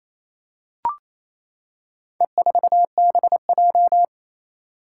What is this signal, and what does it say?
Morse code, heard as a keyed tone, about 28 wpm, 700 Hz. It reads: E4BJ
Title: Four Challenging Letter-Number Combinations 28wpm